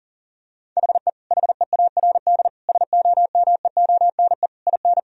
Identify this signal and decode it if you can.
Morse code, heard as a keyed tone, about 40 wpm, 700 Hz. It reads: HE HEARD SOMEONE IN
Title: Sentences from Top 1000 Words - Review Part 4 - 40wpm